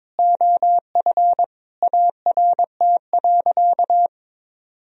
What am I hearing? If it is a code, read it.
Morse code, heard as a keyed tone, about 21 wpm, 700 Hz. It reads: OF ART.
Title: Sentences from Top 1000 Words - Review Part 3 - 22wpm